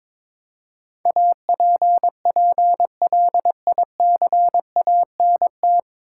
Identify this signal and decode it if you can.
Morse code, heard as a keyed tone, about 22 wpm, 700 Hz. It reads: APPLICANT
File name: 9 Letter Words - First 3 Letters Given - 22wpm